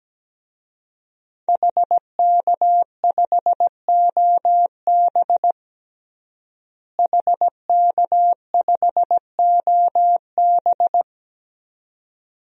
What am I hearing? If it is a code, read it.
Morse code, heard as a keyed tone, about 17 wpm, 700 Hz. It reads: HK5OB HK5OB
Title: DXCC-100 Call Signs - 5 Characters - Repeated 2x - 17wpm